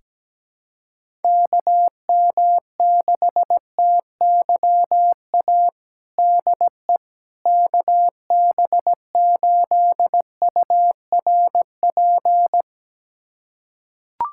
Morse code, heard as a keyed tone, about 17 wpm, 700 Hz. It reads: KM6TYA DE KB8URP
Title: USA Callsign DE Callsign 17wpm